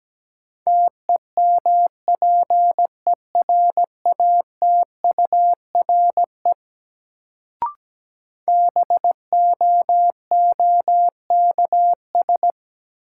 Morse code, heard as a keyed tone, about 17 wpm, 700 Hz. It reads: TEMPERATURE BOOKS
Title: Sentences from Top 1000 Words - Review Part 4 - New Word Drill - 17wpm